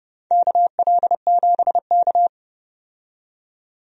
Morse code, heard as a keyed tone, about 31 wpm, 700 Hz. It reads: KL7K
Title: DXCC-100 Call Signs - 4 Characters - 30wpm